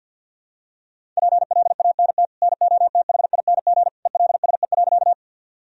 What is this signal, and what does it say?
Morse code, heard as a keyed tone, about 52 wpm, 700 Hz. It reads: PLANT NOTHING ELSE.
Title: Sentences from Top 1000 Words - Review Part 1 - 50wpm